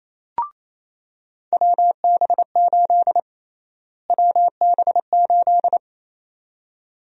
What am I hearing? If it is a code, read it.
Morse code, heard as a keyed tone, about 28 wpm, 700 Hz. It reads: WB8 WB8
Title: USA Call Sign Prefixes - Repeated 2x - 28wpm